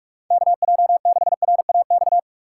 Morse code, heard as a keyed tone, about 48 wpm, 700 Hz. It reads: KJ6RAX
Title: USA Call Signs - 6 Characters - 6x Character Spacing - 45wpm